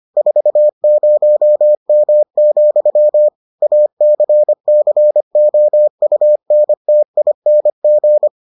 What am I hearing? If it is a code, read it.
Morse code, heard as a keyed tone, about 25 wpm, 600 Hz. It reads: 40M, ACCOUNTING